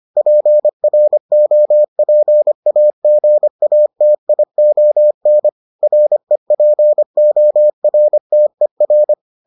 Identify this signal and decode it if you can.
Morse code, heard as a keyed tone, about 25 wpm, 600 Hz. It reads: PROPAGATION REPORTER